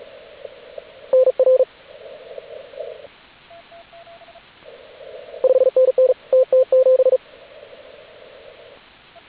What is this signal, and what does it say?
Morse code, heard as a keyed tone, about 36 wpm, 520 Hz, with radio noise. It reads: NR 5NN TT7